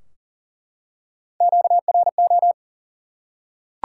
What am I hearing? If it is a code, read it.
Morse code, heard as a keyed tone, about 40 wpm, 700 Hz. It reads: QRO
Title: Top 90 QSO Elements 40wpm